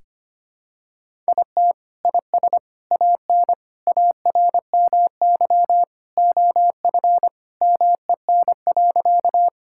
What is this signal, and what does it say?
Morse code, heard as a keyed tone, about 25 wpm, 700 Hz. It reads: IT IS AN ARMY OF MEN.
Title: Sentences from Top 1000 Words - Review Part 5 - 25wpm